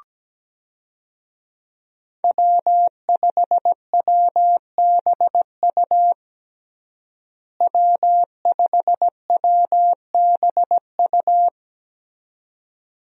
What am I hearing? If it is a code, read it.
Morse code, heard as a keyed tone, about 17 wpm, 700 Hz. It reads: W5WBU W5WBU
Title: USA Call Signs - 5 Characters - Repeated 2x - 17wpm